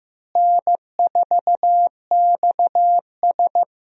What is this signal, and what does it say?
Morse code, heard as a keyed tone, about 15 wpm, 700 Hz. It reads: N4XS
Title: USA Call Signs - 4 Characters - Repeated 2x - 15wpm